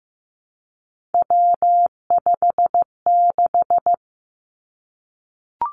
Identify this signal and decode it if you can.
Morse code, heard as a keyed tone, about 15 wpm, 700 Hz. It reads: W56
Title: Three Challenging Letter-Number Combinations 15wpm